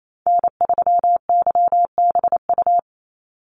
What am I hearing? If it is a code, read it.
Morse code, heard as a keyed tone, about 28 wpm, 700 Hz. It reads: N3YBU
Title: USA Call Signs 28wpm